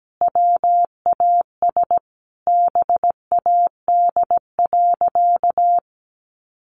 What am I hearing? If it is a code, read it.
Morse code, heard as a keyed tone, about 17 wpm, 700 Hz. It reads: WAS BAD.
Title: Sentences from Top 1500 Words 17wpm